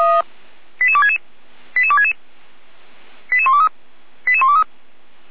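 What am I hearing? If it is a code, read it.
Five-tone selective call (ZVEI), heard as quick runs of five tones, tones of about 70 ms. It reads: 88266 88266 88123 88123
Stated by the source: ZVEI five-tone selective call 88129 received by SDR at feuerwehr-orlamuende.org fire brigade.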